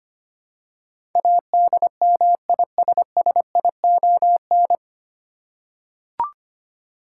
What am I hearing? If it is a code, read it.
Morse code, heard as a keyed tone, about 25 wpm, 700 Hz. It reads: ADMISSION
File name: Nine Letter Words - 25wpm